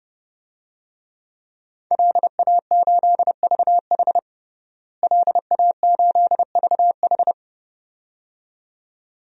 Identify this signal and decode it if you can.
Morse code, heard as a keyed tone, about 32 wpm, 700 Hz. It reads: LA8VH LA8VH
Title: DXCC-100 Call Signs - 5 Characters - Repeated 2x - 30wpm